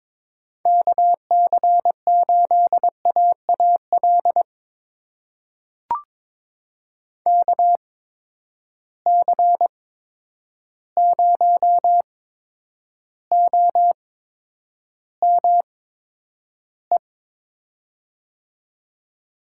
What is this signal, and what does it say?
Morse code, heard as a keyed tone, about 21 wpm, spaced as for 9 wpm, 700 Hz. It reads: KC8AAL KC0OME
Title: USA Call Signs - 6 Characters - 8x Character Spacing - 22wpm